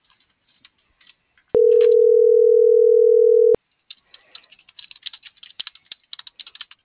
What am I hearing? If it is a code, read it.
Ringback tone (North America), 440 + 480 Hz.